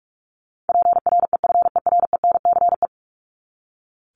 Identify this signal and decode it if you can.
Morse code, heard as a keyed tone, about 44 wpm, 700 Hz. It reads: PREFERENCE
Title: Top 1251-1500 Words 45wpm